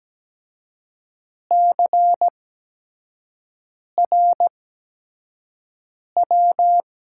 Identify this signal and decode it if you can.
Morse code, heard as a keyed tone, about 17 wpm, spaced as for 5 wpm, 700 Hz. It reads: CRW
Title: USA Call Signs - 6 Characters - 8x Character Spacing - 17wpm